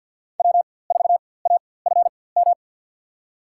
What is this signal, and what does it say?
Morse code, heard as a keyed tone, about 50 wpm, spaced as for 23 wpm, 700 Hz. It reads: W4AFM
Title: USA Call Signs - 5 Characters - 4x Character Spacing - 50wpm